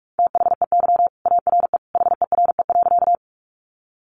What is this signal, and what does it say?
Morse code, heard as a keyed tone, about 44 wpm, 700 Hz. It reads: THEY ARE HERE.
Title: Sentences from Top 200 Words 45wpm